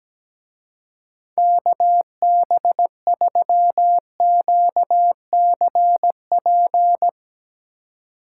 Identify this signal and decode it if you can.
Morse code, heard as a keyed tone, about 17 wpm, 700 Hz. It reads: KB3QCP
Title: Speed Racing - USA 2x3 Call Signs 17wpm